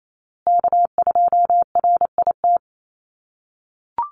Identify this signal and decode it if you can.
Morse code, heard as a keyed tone, about 28 wpm, 700 Hz. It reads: K2RIT
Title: USA Call Signs Repeated 2x 28wpm